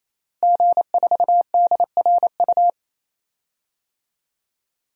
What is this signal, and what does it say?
Morse code, heard as a keyed tone, about 28 wpm, 690 Hz. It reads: G4DRU
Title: DXCC-100 Call Signs - 5 Characters - Repeated 3x - 28wpm